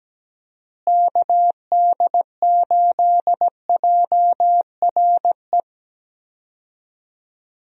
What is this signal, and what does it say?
Morse code, heard as a keyed tone, about 17 wpm, 700 Hz. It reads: KD8JRE
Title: USA Call Signs - 6 Characters - Repeated 3x - 17wpm